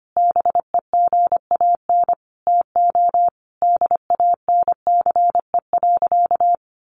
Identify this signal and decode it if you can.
Morse code, heard as a keyed tone, about 25 wpm, 700 Hz. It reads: BEGAN TO DANCE.
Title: Sentences from Top 900 Words 25wpm